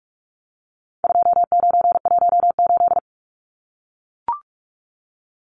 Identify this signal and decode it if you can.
Morse code, heard as a keyed tone, about 44 wpm, 700 Hz. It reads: 2918
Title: Numbers - 4-Digit 45wpm